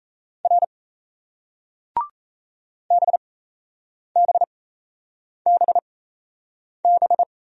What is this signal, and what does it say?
Morse code, heard as a keyed tone, about 37 wpm, spaced as for 8 wpm, 700 Hz. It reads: R BBBB